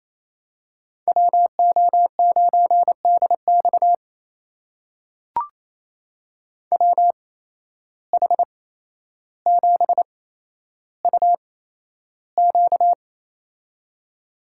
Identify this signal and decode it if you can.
Morse code, heard as a keyed tone, about 28 wpm, spaced as for 12 wpm, 700 Hz. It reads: WO9DX WH7UQ
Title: USA Call Signs - 5 Characters - 8x Character Spacing - 28wpm